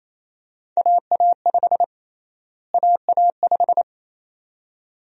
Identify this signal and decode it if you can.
Morse code, heard as a keyed tone, about 28 wpm, 700 Hz. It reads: AA5 AA5
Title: USA Call Sign Prefixes - Repeated 2x - 28wpm